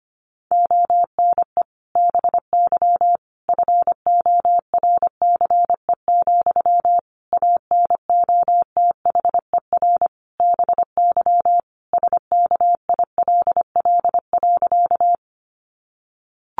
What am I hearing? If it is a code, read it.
Morse code, heard as a keyed tone, about 25 wpm, 700 Hz. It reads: ONE BY FORCE, ANOTHER BY SKILL.
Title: Sentences from Top 1200 Words 25wpm